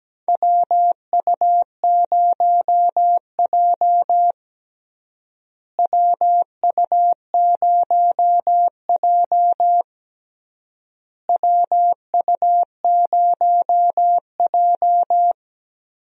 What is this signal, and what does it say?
Morse code, heard as a keyed tone, about 17 wpm, 700 Hz. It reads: WU0J WU0J WU0J
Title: USA Call Signs - 4 Characters - Repeated 3x - 17wpm